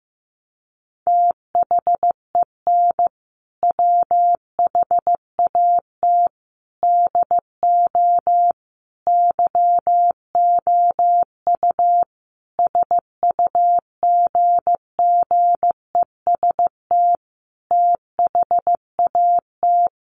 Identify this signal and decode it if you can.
Morse code, heard as a keyed tone, about 15 wpm, 700 Hz. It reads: THEN WHAT DO YOU SUGGEST THAT
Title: Sentences from Top 1300 Words 15wpm